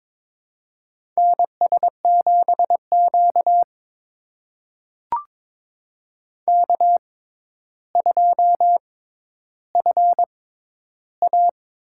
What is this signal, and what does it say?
Morse code, heard as a keyed tone, about 22 wpm, spaced as for 11 wpm, 700 Hz. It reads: NS7Q K2FA
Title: USA Call Signs - 4 Characters - 6x Character Spacing - 22wpm